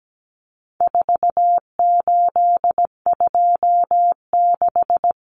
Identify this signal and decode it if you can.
Morse code, heard as a keyed tone, about 17 wpm, 700 Hz. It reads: 4826
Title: Numbers - 4-Digit 17wpm